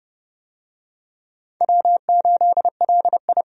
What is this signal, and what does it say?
Morse code, heard as a keyed tone, about 31 wpm, 700 Hz. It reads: W8LI
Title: USA Call Signs - 4 Characters - 30wpm